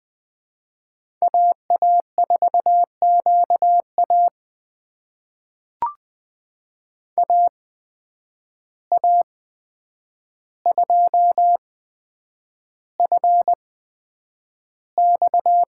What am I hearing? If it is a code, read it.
Morse code, heard as a keyed tone, about 20 wpm, spaced as for 9 wpm, 700 Hz. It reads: AA4QA AA2FX
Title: USA Call Signs - 5 Characters - 8x Character Spacing - 20wpm